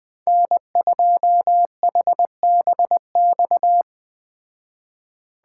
Morse code, heard as a keyed tone, about 20 wpm, 700 Hz. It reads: N2HBX
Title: Speed Racing - USA 1x3 Call Signs 17wpm